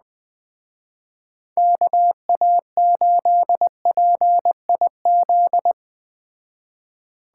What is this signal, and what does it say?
Morse code, heard as a keyed tone, about 21 wpm, 700 Hz. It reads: KA8PIZ
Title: Speed Racing - USA 2x3 Call Signs 17wpm